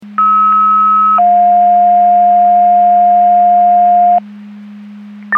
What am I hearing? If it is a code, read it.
Two-tone page, heard as a one-second tone, then a three-second tone, about 1290 then 710 Hz. It is heard from 0.2 to 4.2 s.
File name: Portage 154.385 Paging